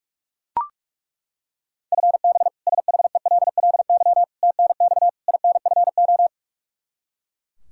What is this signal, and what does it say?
Morse code, heard as a keyed tone, about 42 wpm, 700 Hz. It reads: FB SHELLY TNX INFO
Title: K1USN SST QSO Practice 45wpm